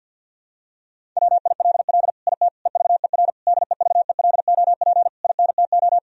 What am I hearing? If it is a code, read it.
Morse code, heard as a keyed tone, about 50 wpm, 700 Hz. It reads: WILL IT EVER DEVELOP INTO
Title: Sentences from Top 1000 Words - Review Part 2 - 50wpm